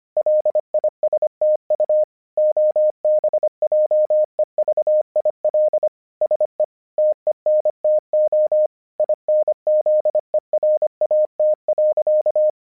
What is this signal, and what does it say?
Morse code, heard as a keyed tone, about 25 wpm, 610 Hz. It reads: LISTU OBJEVIL SE TENTO INZERAT.
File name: vyuka cw 1 / 25wpm / hasek 25wpm uvod